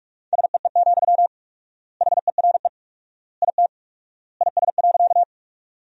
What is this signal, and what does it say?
Morse code, heard as a keyed tone, about 41 wpm, 700 Hz. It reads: SEE, HERE IT IS.
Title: 4x Word-Spacing - Sentences from Top 200 Words - 45wpm